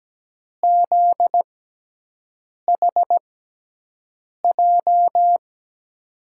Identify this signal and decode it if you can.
Morse code, heard as a keyed tone, about 17 wpm, spaced as for 6 wpm, 700 Hz. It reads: ZHJ
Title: USA Call Signs - 6 Characters - 6x Character Spacing - 17wpm